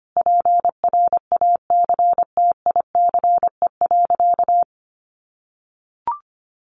Morse code, heard as a keyed tone, about 25 wpm, 700 Hz. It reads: PRACTICE.
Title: Sentences from Top 900 Words 25wpm